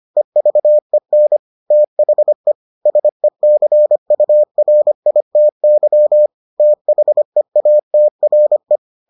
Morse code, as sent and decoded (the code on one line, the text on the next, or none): . ...- . -. / - .... / ... -.-. ..- .-. .. - -.-- / - .... .- - .-. .
EVEN THE SECURITY THEATRE